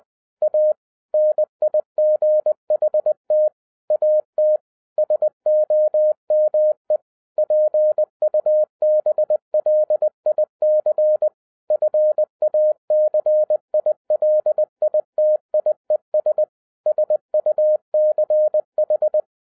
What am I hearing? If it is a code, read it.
Morse code, heard as a keyed tone, about 20 wpm, 610 Hz. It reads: A NIGHT AT SOME PUBLIC FACILITIES SUCH
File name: 2011-07-mp3 / n33 THE 3.11 2 20wpm